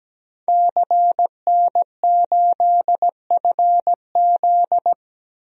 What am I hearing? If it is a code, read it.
Morse code, heard as a keyed tone, about 17 wpm, 700 Hz. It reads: CN8FZ
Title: DXCC-100 Call Signs - 5 Characters - 17wpm